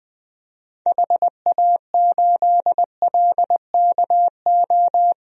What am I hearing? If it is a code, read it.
Morse code, heard as a keyed tone, about 20 wpm, 700 Hz. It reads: HA8LKO
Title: DXCC-100 Call Signs - 6 Characters - 20wpm